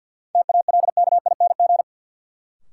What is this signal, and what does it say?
Morse code, heard as a keyed tone, about 53 wpm, 700 Hz. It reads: TALKING
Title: Sets of 2 Words - New Word Drill - 50wpm